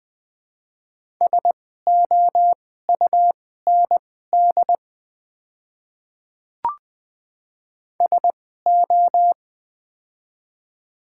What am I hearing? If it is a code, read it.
Morse code, heard as a keyed tone, about 20 wpm, spaced as for 14 wpm, 700 Hz. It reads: SOUND SO
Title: 2x Character Spacing - Top 100 Words - 20wpm